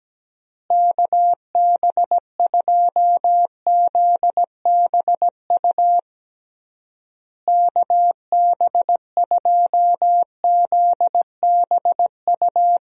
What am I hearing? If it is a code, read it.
Morse code, heard as a keyed tone, about 17 wpm, 700 Hz. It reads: KB2ZBU KB2ZBU
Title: USA Call Signs - 6 Characters - Repeated 3x - 17wpm